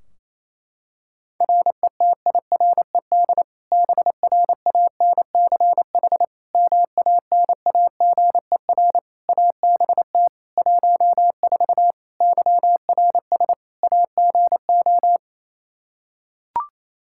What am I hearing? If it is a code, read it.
Morse code, heard as a keyed tone, about 28 wpm, 700 Hz. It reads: RETIRED BRANCH MANAGER ABT 14 YRS AGO
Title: Practice QSO - Exchange 3 - More Info - 28wpm